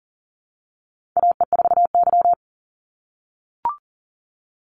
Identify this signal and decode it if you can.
Morse code, heard as a keyed tone, about 40 wpm, 700 Hz. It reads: AE4Y
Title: NA SOTA Chaser Callsigns 40wpm